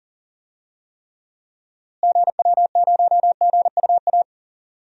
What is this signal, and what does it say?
Morse code, heard as a keyed tone, about 40 wpm, 700 Hz. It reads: GW0GUA
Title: DXCC-100 Call Signs - 6 Characters - Repeated 2x - 40wpm